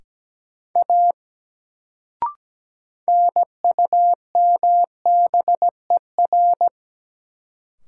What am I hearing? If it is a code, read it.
Morse code, heard as a keyed tone, about 17 wpm, 700 Hz. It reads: A NUMBER
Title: Sentences from Top 100 Words - New Word Drill - 17wpm